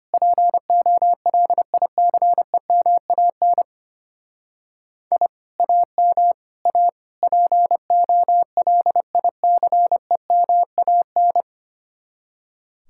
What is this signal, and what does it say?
Morse code, heard as a keyed tone, about 32 wpm, 700 Hz. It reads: POLICEMAN I AM A POLICEMAN